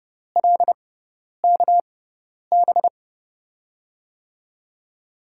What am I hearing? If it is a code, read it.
Morse code, heard as a keyed tone, about 30 wpm, spaced as for 10 wpm, 700 Hz. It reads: LKB
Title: USA Call Signs - 6x Character Spacing - 30wpm